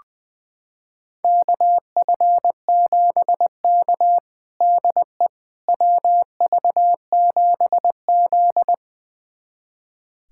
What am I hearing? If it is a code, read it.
Morse code, heard as a keyed tone, about 21 wpm, 700 Hz. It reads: KF7K DE WV7Z
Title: USA Callsign DE Callsign - 4 Characters - 20wpm